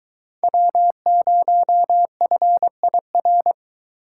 Morse code, heard as a keyed tone, about 22 wpm, 700 Hz. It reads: W0FIR